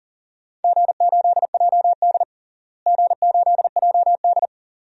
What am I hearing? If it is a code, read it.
Morse code, heard as a keyed tone, about 40 wpm, 700 Hz. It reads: G8JD G8JD
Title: DXCC-100 Call Signs - 4 Characters - Repeated 2x - 40wpm